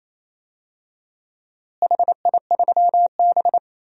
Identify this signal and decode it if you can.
Morse code, heard as a keyed tone, about 28 wpm, 700 Hz. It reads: HI3B
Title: DXCC-100 Call Signs - 4 Characters - Repeated 2x - 28wpm